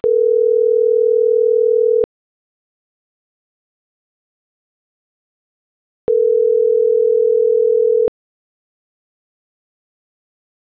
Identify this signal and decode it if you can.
Ringback tone (North America), 440 + 480 Hz.